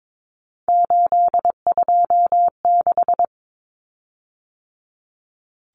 Morse code, heard as a keyed tone, about 22 wpm, 700 Hz. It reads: 826
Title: Common Cut Numbers - 3-Digit 22wpm